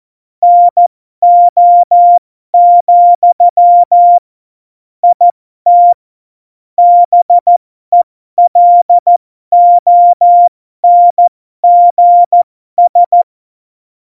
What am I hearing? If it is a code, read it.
Morse code, heard as a keyed tone, about 14 wpm, spaced as for 12 wpm, 700 Hz. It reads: NO, IT BELONGS
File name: Morsekurs / Witze